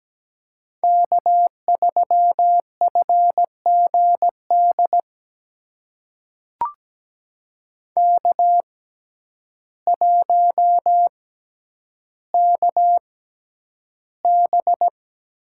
Morse code, heard as a keyed tone, about 17 wpm, spaced as for 9 wpm, 700 Hz. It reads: K3FGD K1KB
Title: USA Call Signs - 6x Character Spacing - 17wpm